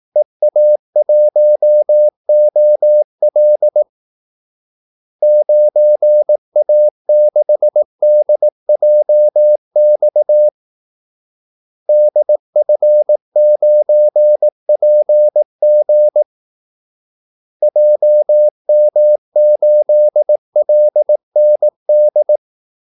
Morse code, heard as a keyed tone, about 18 wpm, 600 Hz. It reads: EA1OL 9A6DJX DF9PG JM8LND
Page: Kaldesignaler 18 wpm